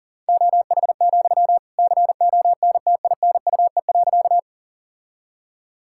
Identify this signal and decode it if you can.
Morse code, heard as a keyed tone, about 40 wpm, 700 Hz. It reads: OH, CONTINUE.
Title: Sentences from Top 1000 Words - Review Part 6 - 40wpm